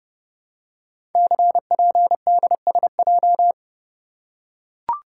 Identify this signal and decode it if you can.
Morse code, heard as a keyed tone, about 28 wpm, 700 Hz. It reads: CPDSJ
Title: Code Groups - 5 Letter:Numbers - 30wpm